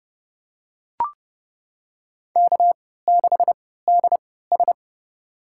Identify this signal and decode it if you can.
Morse code, heard as a keyed tone, about 30 wpm, spaced as for 17 wpm, 700 Hz. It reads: K6DS
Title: USA Call Signs - 4 Characters - 3x Character Spacing - 30wpm